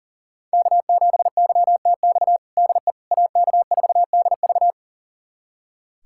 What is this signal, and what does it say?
Morse code, heard as a keyed tone, about 40 wpm, 700 Hz. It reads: K7YTX DE AK4DV
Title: USA Callsign DE Callsign - 5 Characters - 40wpm